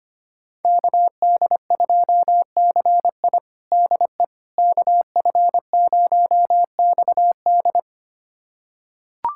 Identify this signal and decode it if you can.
Morse code, heard as a keyed tone, about 25 wpm, 700 Hz. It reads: KD2CI DE KF0XD
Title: USA Callsign DE Callsign - 5 Characters - 25wpm